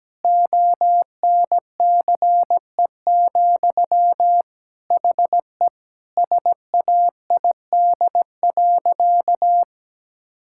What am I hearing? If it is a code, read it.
Morse code, heard as a keyed tone, about 17 wpm, 700 Hz. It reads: ONCE, HE SAID.
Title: Sentences from Top 1600 Words 17wpm